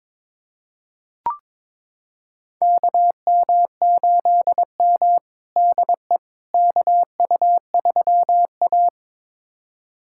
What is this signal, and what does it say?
Morse code, heard as a keyed tone, about 22 wpm, 700 Hz. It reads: KM8M DE KU3A
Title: USA Callsign DE Callsign - 4 Characters - 22wpm